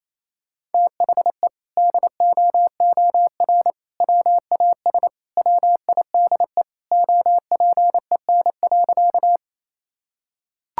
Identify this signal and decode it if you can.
Morse code, heard as a keyed tone, about 28 wpm, 700 Hz. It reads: THE DOOR WAS WIDE OPEN.
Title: Sentences from Top 1000 Words - Review Part 6 - 28wpm